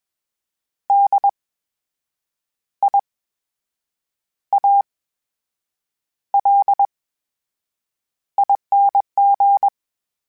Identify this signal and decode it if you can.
Morse code, heard as a keyed tone, about 21 wpm, spaced as for 7 wpm, 800 Hz. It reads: DIALING
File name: Special INGs-mp3s / Special INGs-mp3s / VerySlowToFaster Ing @20